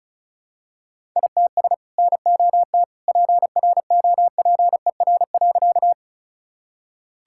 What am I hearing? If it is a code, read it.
Morse code, heard as a keyed tone, about 34 wpm, 700 Hz. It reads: ITS NOT PROPER.
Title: Sentences from Top 1000 Words - Review Part 5 - 35wpm